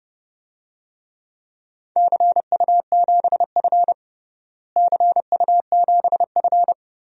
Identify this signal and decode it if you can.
Morse code, heard as a keyed tone, about 32 wpm, 700 Hz. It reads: CU7F CU7F
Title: DXCC-100 Call Signs - 4 Characters - Repeated 2x - 30wpm